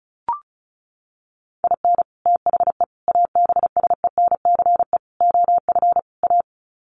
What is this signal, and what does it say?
Morse code, heard as a keyed tone, about 34 wpm, 700 Hz. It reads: IN THE ABSENCE OF A
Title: Sets of 5 Words Encore 35wpm